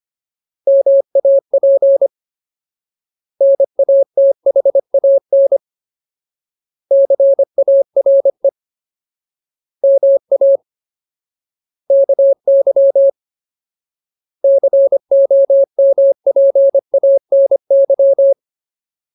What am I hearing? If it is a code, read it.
Morse code, heard as a keyed tone, about 25 wpm, 550 Hz. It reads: MAP NATHAN CARE MA KY COMPANY
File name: Ultimate Word List / 25WPM Ultimate